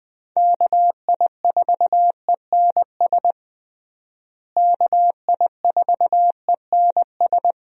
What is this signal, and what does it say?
Morse code, heard as a keyed tone, about 21 wpm, 700 Hz. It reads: KI4ENS KI4ENS
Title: USA Call Signs - 6 Characters - Repeated 3x - 20wpm